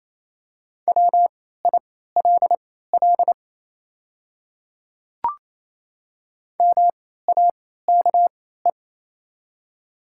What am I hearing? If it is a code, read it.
Morse code, heard as a keyed tone, about 28 wpm, spaced as for 16 wpm, 700 Hz. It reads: WILL MAKE
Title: 3x Character Spacing - Top 100 Words - 28wpm